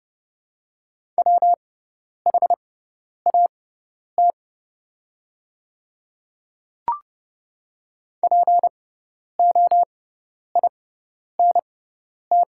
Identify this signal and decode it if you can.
Morse code, heard as a keyed tone, about 30 wpm, spaced as for 10 wpm, 700 Hz. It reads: WHAT POINT
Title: 6x Character Spacing - Top 300 Words - 30wpm